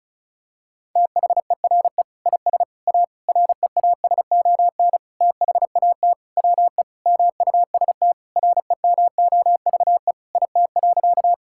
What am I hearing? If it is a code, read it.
Morse code, heard as a keyed tone, about 36 wpm, 690 Hz. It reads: THERE IS A REASON THAT WE MUST REMOVE IT.
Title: Sentences from Top 1000 Words - Review Part 5 - 35wpm